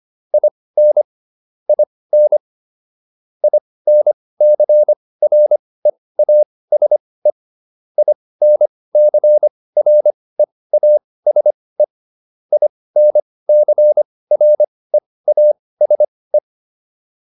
Morse code, heard as a keyed tone, about 25 wpm, spaced as for 18 wpm, 600 Hz. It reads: IN IN INCREASE INCREASE INCREASE